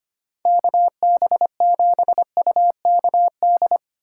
Morse code, heard as a keyed tone, about 31 wpm, 700 Hz. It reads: KB7UKD